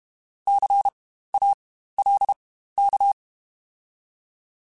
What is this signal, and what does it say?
Morse code, heard as a keyed tone, about 32 wpm, spaced as for 15 wpm, 800 Hz. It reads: CALK
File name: Spaced Alphas / Spaced Alphas mp3s / Cs sp @30